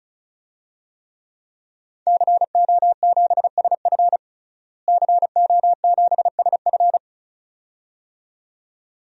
Morse code, heard as a keyed tone, about 36 wpm, 700 Hz. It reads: CO7SF CO7SF
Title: DXCC-100 Call Signs - 5 Characters - Repeated 2x - 35wpm